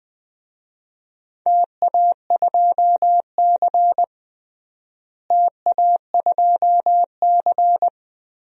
Morse code, heard as a keyed tone, about 21 wpm, 700 Hz. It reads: TA2C TA2C
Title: DXCC-100 Call Signs - 4 Characters - Repeated 3x - 20wpm